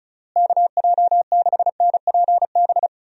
Morse code, heard as a keyed tone, about 34 wpm, 700 Hz. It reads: KJ6NPB
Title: USA Call Signs - 6 Characters - Repeated 3x - 35wpm